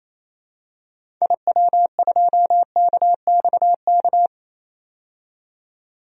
Morse code, heard as a keyed tone, about 27 wpm, 700 Hz. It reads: IW2KXK
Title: DXCC-100 Call Signs - 6 Characters - 28wpm